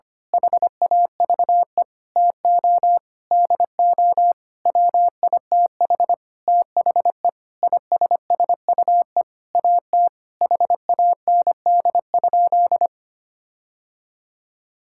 Morse code, heard as a keyed tone, about 25 wpm, 700 Hz. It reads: HAVE TO DO WITH THE ISSUE AT HAND?
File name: practice / 25wpm / Sentences with 9-Word Phrases - 25wpm